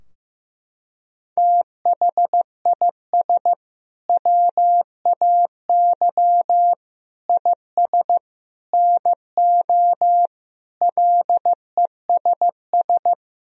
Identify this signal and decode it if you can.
Morse code, heard as a keyed tone, about 15 wpm, 700 Hz. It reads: THIS WAY IS NO LESS
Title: Sentences from Top 1000 Words - Review Part 2 - 15wpm